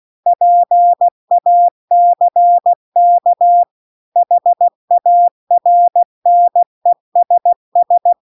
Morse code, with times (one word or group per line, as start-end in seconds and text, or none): 0.3-3.6 PACK
4.2-8.1 HARNESS